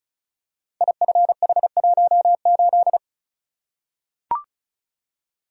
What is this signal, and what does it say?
Morse code, heard as a keyed tone, about 36 wpm, 700 Hz. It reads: IFH18
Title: Code Groups - 5 Letter:Numbers - 35wpm